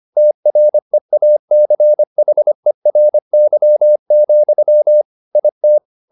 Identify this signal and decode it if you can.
Morse code, heard as a keyed tone, about 25 wpm, 600 Hz. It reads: TREACHERY, IT
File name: the old man and the sea part 1 25wpm